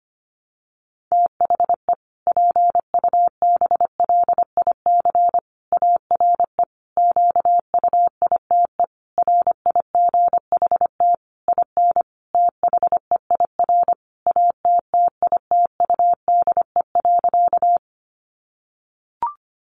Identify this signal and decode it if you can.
Morse code, heard as a keyed tone, about 25 wpm, 700 Hz. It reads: THE PUBLIC ARE QUITE RIGHT IN THEIR ATTITUDE.
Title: Sentences from Top 1300 Words 25wpm